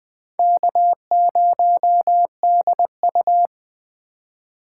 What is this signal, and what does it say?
Morse code, heard as a keyed tone, about 24 wpm, 700 Hz. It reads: K0DU
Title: Speed Racing - USA 1x2 Call Signs 20wpm